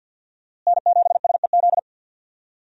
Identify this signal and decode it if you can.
Morse code, heard as a keyed tone, about 52 wpm, 700 Hz. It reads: N7SEZ
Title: USA Call Signs - 5 Characters - Repeated 2x - 50wpm